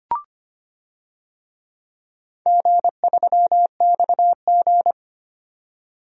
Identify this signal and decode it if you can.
Morse code, heard as a keyed tone, about 25 wpm, 700 Hz. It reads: G3XG
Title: DXCC-100 Call Signs - 4 Characters - 25wpm